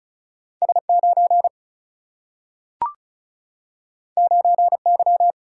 Morse code, heard as a keyed tone, about 36 wpm, 700 Hz. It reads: S9 9Y
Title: Single-Letter Single-Number 35wpm